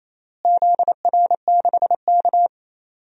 Morse code, heard as a keyed tone, about 28 wpm, 700 Hz. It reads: ZR6K
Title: DXCC-100 Call Signs - 4 Characters - Repeated 3x - 28wpm